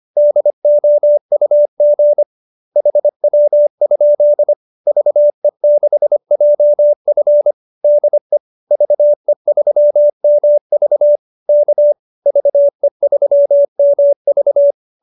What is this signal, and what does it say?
Morse code, heard as a keyed tone, about 25 wpm, 590 Hz. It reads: DOUG HW? VE6JF DE VE3MV K VE3MV
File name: QSO303 25